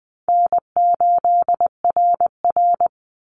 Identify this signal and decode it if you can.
Morse code, heard as a keyed tone, about 20 wpm, 700 Hz. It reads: N8RR
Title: Easy Call Signs Repeated 3x - 20wpm